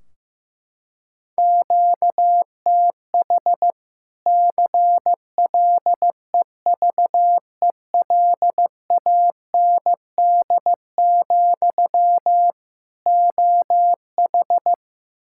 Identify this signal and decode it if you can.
Morse code, heard as a keyed tone, about 15 wpm, 700 Hz. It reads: QTH CLEVELAND, OH
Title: Practice QSO - Exchange 1 - Beginning - 15wpm